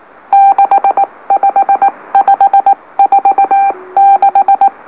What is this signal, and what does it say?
Morse code, heard as a keyed tone, about 19 wpm, spaced as for 16 wpm, 790 Hz. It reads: 65546
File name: M04- -2345utc-LOLO